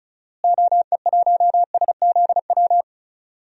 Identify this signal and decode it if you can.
Morse code, heard as a keyed tone, about 34 wpm, 700 Hz. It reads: OE1SZW
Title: DXCC-100 Call Signs - 6 Characters - Repeated 3x - 35wpm